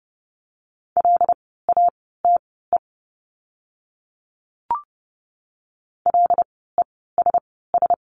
Morse code, heard as a keyed tone, about 30 wpm, spaced as for 17 wpm, 700 Hz. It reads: LATE LESS
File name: Lesson 16f - Letter c - Words 30wpm